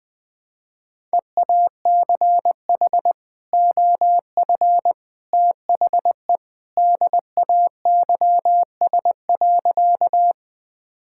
Morse code, heard as a keyed tone, about 21 wpm, 700 Hz. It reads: EACH OF THE DAYS.
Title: Sentences with 3-Word Phrases - Spoken Beginning - 20wpm